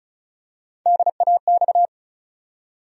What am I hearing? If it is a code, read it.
Morse code, heard as a keyed tone, about 36 wpm, 700 Hz. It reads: DAX
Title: USA Call Sign Postfixes - 3 Letters 35wpm